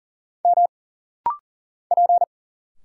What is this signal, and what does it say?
Morse code, heard as a keyed tone, about 40 wpm, 700 Hz. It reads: M P
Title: Single Letters - Rapid-Fire 40wpm